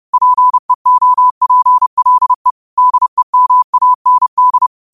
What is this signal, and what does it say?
Morse code, heard as a keyed tone, about 30 wpm, 1000 Hz. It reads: PEOPLE DEMAND
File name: qotdfiles / 30wpm